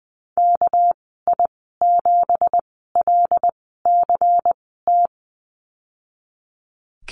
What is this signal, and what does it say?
Morse code, heard as a keyed tone, about 20 wpm, spaced as for 15 wpm, 700 Hz. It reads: KI7LCT
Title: 219 Lesson 35g - Number 0 - Call Signs 20wpm